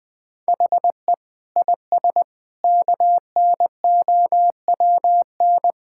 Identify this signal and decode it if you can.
Morse code, heard as a keyed tone, about 20 wpm, 700 Hz. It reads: HE IS KNOWN
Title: Sentences with 5-Word Phrases - 20wpm